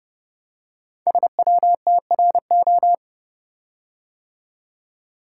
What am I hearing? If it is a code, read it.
Morse code, heard as a keyed tone, about 29 wpm, 700 Hz. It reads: SWTRO
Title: Code Groups - 5 Letters - 30wpm